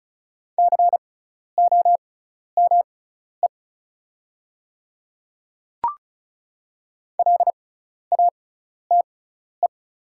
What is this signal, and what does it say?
Morse code, heard as a keyed tone, about 36 wpm, spaced as for 12 wpm, 700 Hz. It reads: COME LATE
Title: 6x Character Spacing - Top 500 Words - 35wpm